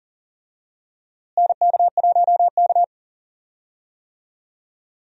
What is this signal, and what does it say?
Morse code, heard as a keyed tone, about 38 wpm, 700 Hz. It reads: NK1K
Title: USA Call Signs - 4 Characters - Repeated 2x - 40wpm